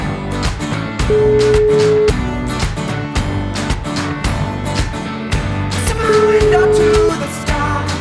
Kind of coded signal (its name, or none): telephone call progress tone